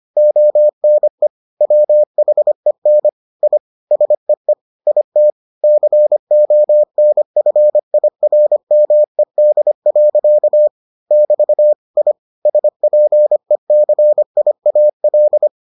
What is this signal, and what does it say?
Morse code, heard as a keyed tone, about 25 wpm, 600 Hz. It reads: ONE WHEN I SEE IT CONFIRMED. <BT> I SPECIAL